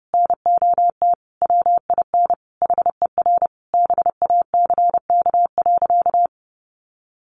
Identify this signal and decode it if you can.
Morse code, heard as a keyed tone, about 30 wpm, 700 Hz. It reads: NOT WIN HER BACK.
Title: Sentences from Top 1500 Words 30wpm